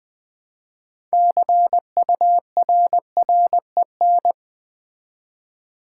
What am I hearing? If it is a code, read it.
Morse code, heard as a keyed tone, about 20 wpm, 700 Hz. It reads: CURREN
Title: Build A Word - 3 Letters At A Time - 9-Letter Words - 20wpm